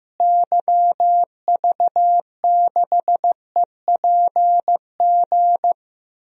Morse code, heard as a keyed tone, about 16 wpm, 700 Hz. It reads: YV6EPG
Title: DXCC-100 Call Signs - 6 Characters - Repeated 2x - 15wpm